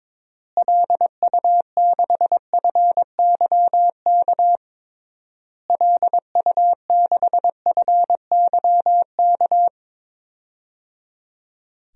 Morse code, heard as a keyed tone, about 22 wpm, 700 Hz. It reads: LU6FYK LU6FYK
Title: DXCC-100 Call Signs - 6 Characters - Repeated 3x - 22wpm